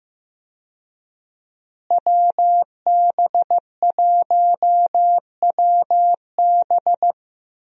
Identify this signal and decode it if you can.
Morse code, heard as a keyed tone, about 15 wpm, 700 Hz. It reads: WB1WB
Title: USA Call Signs - 5 Characters - Repeated 2x - 15wpm